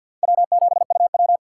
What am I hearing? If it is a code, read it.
Morse code, heard as a keyed tone, about 52 wpm, 700 Hz. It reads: W7UW
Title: USA Call Signs - 4 Characters - 3x Character Spacing - 50wpm